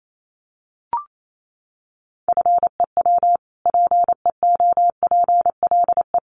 Morse code, heard as a keyed tone, about 28 wpm, 700 Hz. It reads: FEW PEOPLE
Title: Sentences from Top 600 Words 28wpm